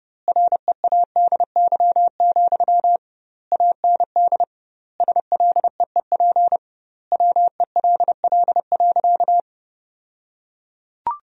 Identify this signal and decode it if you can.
Morse code, heard as a keyed tone, about 30 wpm, 700 Hz. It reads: READY, AND SLEEP WELL.
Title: 2x Word-Spacing - Sentences from Top 500 Words - 30wpm